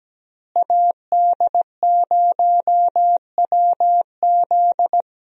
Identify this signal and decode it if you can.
Morse code, heard as a keyed tone, about 17 wpm, 700 Hz. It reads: AD0WZ
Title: USA Call Signs - 5 Characters - Repeated 2x - 17wpm